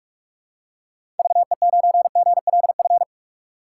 Morse code, heard as a keyed tone, about 41 wpm, 700 Hz. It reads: VE9GLF
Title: K1USN SST QSO Practice - 45wpm